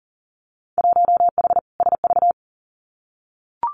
Morse code, heard as a keyed tone, about 41 wpm, 700 Hz. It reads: 1H SV
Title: Winter Field Day Exchanges 40wpm